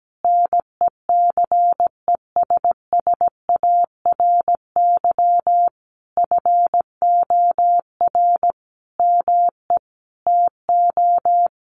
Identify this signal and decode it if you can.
Morse code, heard as a keyed tone, about 17 wpm, 700 Hz. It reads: NECESSARY FOR ME TO
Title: Sentences from Top 1300 Words 17wpm